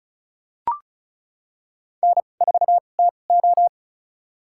Morse code, heard as a keyed tone, about 34 wpm, spaced as for 25 wpm, 700 Hz. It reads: N4TO
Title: USA Call Signs - 2x Character Spacing - 35wpm